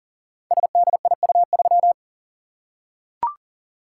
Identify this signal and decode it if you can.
Morse code, heard as a keyed tone, about 42 wpm, 700 Hz. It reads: SDIU3
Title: Code Groups - 5 Letter:Numbers - 40wpm